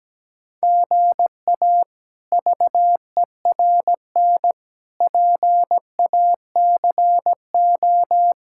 Morse code, heard as a keyed tone, about 17 wpm, 700 Hz. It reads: GA VERN PACO
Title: K1USN SST QSO Practice - 17wpm